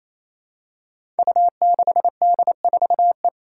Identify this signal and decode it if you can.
Morse code, heard as a keyed tone, about 27 wpm, 700 Hz. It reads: U6D4E
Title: Five Challenging Letter-Number Combinations 28wpm